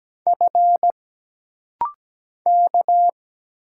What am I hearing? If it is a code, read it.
Morse code, heard as a keyed tone, about 17 wpm, 700 Hz. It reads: F K
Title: Single Letters - Rapid-Fire 17wpm